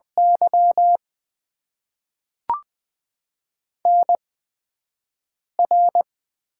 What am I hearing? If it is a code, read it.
Morse code, heard as a keyed tone, about 20 wpm, spaced as for 5 wpm, 700 Hz. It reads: Y NR
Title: Top 90 QSO Elements - 8x Character Spacing - 20wpm